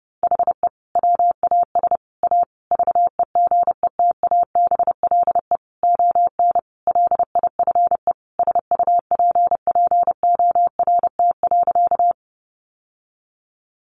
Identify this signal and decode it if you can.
Morse code, heard as a keyed tone, about 30 wpm, 700 Hz. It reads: HE WAS A VEGETABLE ON LIFE SUPPORT.
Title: Sentences from Top 1300 Words 30wpm